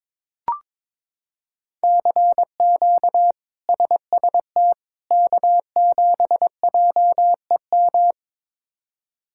Morse code, heard as a keyed tone, about 22 wpm, 700 Hz. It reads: CQ SST K7JEM
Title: K1USN SST QSO Practice - 22wpm